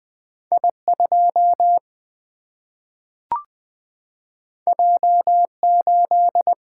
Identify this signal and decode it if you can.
Morse code, heard as a keyed tone, about 20 wpm, 700 Hz. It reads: I2 J8
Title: Single-Letter Single-Number 20wpm